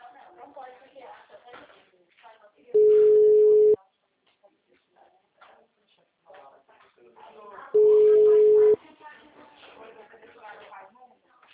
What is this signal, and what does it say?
Ringback tone (Europe, 425 Hz), with speech.
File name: out- -2025- - - .29035